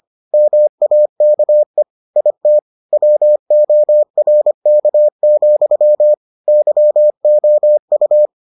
Morse code, heard as a keyed tone, about 25 wpm, 600 Hz. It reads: MAKE IT WORK, YOU